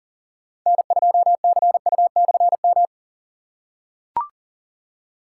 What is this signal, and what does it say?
Morse code, heard as a keyed tone, about 40 wpm, 700 Hz. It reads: N2CU/M
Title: K1USN SST QSO Practice - 40wpm